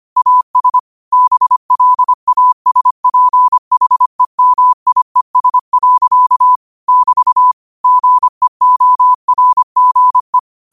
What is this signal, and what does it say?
Morse code, heard as a keyed tone, about 25 wpm, 1000 Hz. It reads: AS BLASPHEMIES. <BT> GEORGE